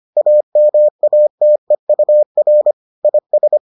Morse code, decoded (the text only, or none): AMATEUR IS